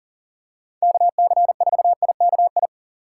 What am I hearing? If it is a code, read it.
Morse code, heard as a keyed tone, about 40 wpm, 700 Hz. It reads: KC4IKI
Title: USA Call Signs - 6 Characters - 3x Character Spacing - 40wpm